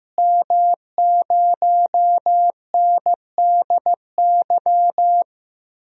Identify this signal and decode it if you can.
Morse code, heard as a keyed tone, about 15 wpm, 700 Hz. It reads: M0NDY
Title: DXCC-100 Call Signs - 5 Characters - Repeated 3x - 15wpm